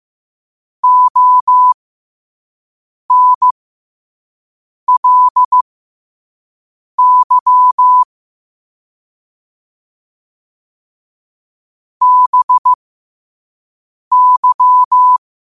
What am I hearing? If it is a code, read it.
Morse code, heard as a keyed tone, about 15 wpm, spaced as for 5 wpm, 1000 Hz. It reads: ONLY BY
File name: qotdfiles / 5wpm